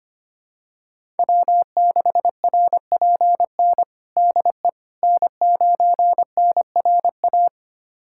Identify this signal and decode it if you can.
Morse code, heard as a keyed tone, about 25 wpm, 700 Hz. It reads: W6RPN DE N9NRA
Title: USA Callsign DE Callsign - 5 Characters - 25wpm